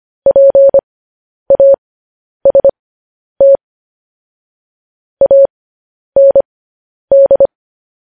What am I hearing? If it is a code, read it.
Morse code, heard as a keyed tone, about 25 wpm, spaced as for 10 wpm, 550 Hz. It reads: PAST AND
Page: news-headlines-in-morse-code-at-10-wpm